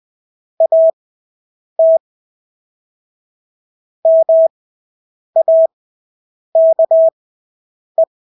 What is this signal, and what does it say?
Morse code, heard as a keyed tone, about 20 wpm, spaced as for 8 wpm, 650 Hz. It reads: AT MAKE
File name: Engelske ord 20/8 wpm